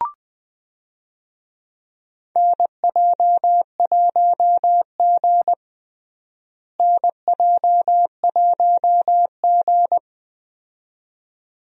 Morse code, heard as a keyed tone, about 20 wpm, 700 Hz. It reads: NJ1G NJ1G
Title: USA Call Signs - 4 Characters - Repeated 2x - 20wpm